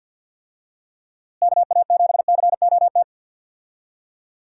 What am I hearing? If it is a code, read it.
Morse code, heard as a keyed tone, about 47 wpm, 700 Hz. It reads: KA7COT
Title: USA Call Signs - 6 Characters - Repeated 2x - 50wpm